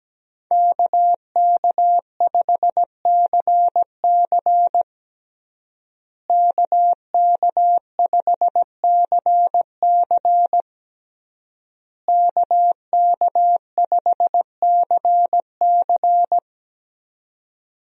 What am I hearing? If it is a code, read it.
Morse code, heard as a keyed tone, about 17 wpm, 700 Hz. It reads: KK5CC KK5CC KK5CC
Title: USA Call Signs - 5 Characters - Repeated 3x - 17wpm